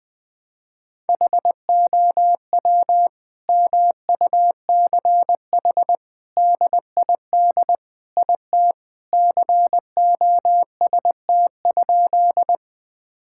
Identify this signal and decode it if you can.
Morse code, heard as a keyed tone, about 21 wpm, 700 Hz. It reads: HOW MUCH DID IT COST?
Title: Sentences from Top 1000 Words - Review Part 5 - 20wpm